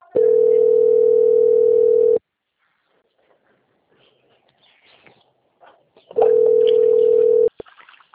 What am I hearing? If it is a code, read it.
Ringback tone (North America), 440 + 480 Hz.
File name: out- - - - - .8273